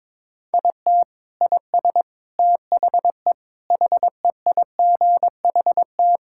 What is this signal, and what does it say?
Morse code, heard as a keyed tone, about 22 wpm, 700 Hz. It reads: IT IS THE HEIGHT
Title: Sentences from Top 1600 Words 22wpm